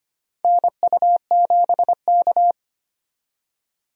Morse code, heard as a keyed tone, about 25 wpm, 700 Hz. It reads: NU7K
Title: USA Call Signs - 4 Characters - 25wpm